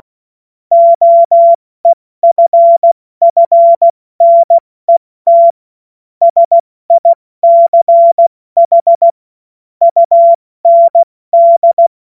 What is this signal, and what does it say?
Morse code, heard as a keyed tone, about 16 wpm, spaced as for 14 wpm, 680 Hz. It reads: OEFFNET SICH UND
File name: Morsekurs / Witze / Witz 07. Dezember 2021, 70 F80